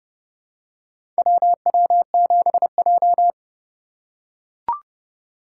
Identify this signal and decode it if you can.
Morse code, heard as a keyed tone, about 30 wpm, 700 Hz. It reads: WW7J
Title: USA Call Signs - 4x Character Spacing - 30wpm